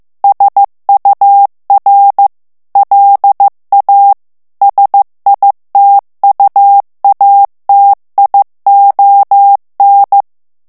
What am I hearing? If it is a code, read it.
Morse code, heard as a keyed tone, about 15 wpm, 800 Hz. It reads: SUR LA SITUATION
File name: morse mp3 / FRANCAIS TEXTE CLAIR 1 VITESSE 14 MOTS MINUTES